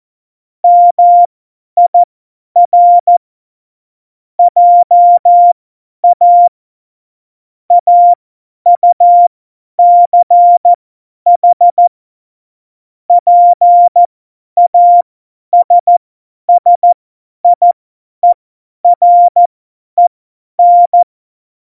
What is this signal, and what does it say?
Morse code, heard as a keyed tone, about 14 wpm, spaced as for 10 wpm, 700 Hz. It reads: MIR JA AUCH PASSIEREN
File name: Morsekurs / Witze / Witz 11. Dezember, 50 F70